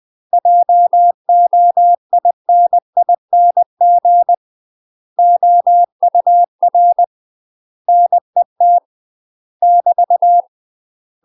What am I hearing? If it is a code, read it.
Morse code, heard as a keyed tone, about 20 wpm, 700 Hz. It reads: JOINING OUR NET <BT>
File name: MASTER QSOs TRAINING/20WPM